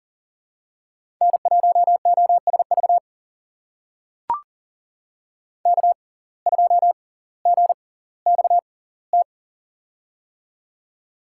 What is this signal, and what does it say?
Morse code, heard as a keyed tone, about 40 wpm, spaced as for 20 wpm, 700 Hz. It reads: N1OSV K2GXT
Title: USA Call Signs - 5 Characters - 6x Character Spacing - 40wpm